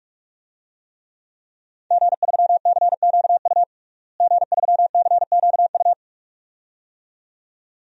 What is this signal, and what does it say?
Morse code, heard as a keyed tone, about 46 wpm, 700 Hz. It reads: G3CQU G3CQU
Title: DXCC-100 Call Signs - 5 Characters - Repeated 2x - 45wpm